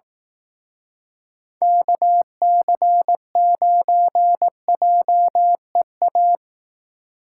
Morse code, heard as a keyed tone, about 18 wpm, 700 Hz. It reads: KC9JEA
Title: Speed Racing - USA 2x3 Call Signs 15wpm